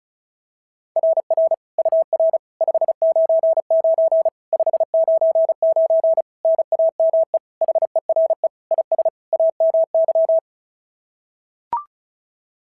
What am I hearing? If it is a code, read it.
Morse code, heard as a keyed tone, about 33 wpm, 650 Hz. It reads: RR UR 599 599 NAME HERE IS AMY
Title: SOTA QSO Practice 35wpm